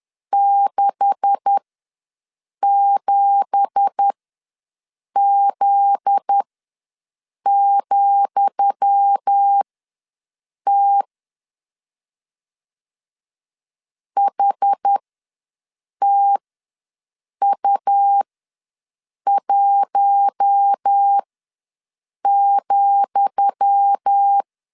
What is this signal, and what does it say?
Morse code, heard as a keyed tone, about 10 wpm, spaced as for 6 wpm, 800 Hz. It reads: 67Z,T HTU1,
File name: LEZIONI / 10 SIGNS / 10D COMA ws 10wpm 4314